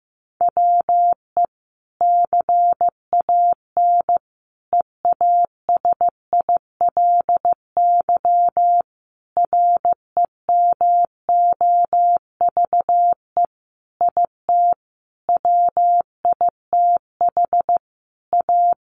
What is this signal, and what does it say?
Morse code, heard as a keyed tone, about 15 wpm, 700 Hz. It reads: WE CAN EASILY REMOVE IT WITH A
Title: Sentences from Top 1200 Words 15wpm